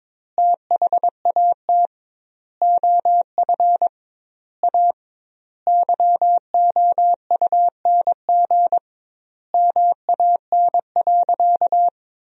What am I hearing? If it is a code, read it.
Morse code, heard as a keyed tone, about 22 wpm, 700 Hz. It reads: THAT OF A YOUNG MAN.
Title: 2x Word-Spacing - Sentences from Top 500 Words - 22wpm